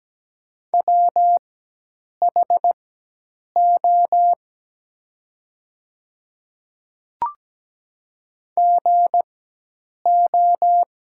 Morse code, heard as a keyed tone, about 17 wpm, spaced as for 8 wpm, 700 Hz. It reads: WHO GO
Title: 4x Character Spacing - Top 100 Words - 17wpm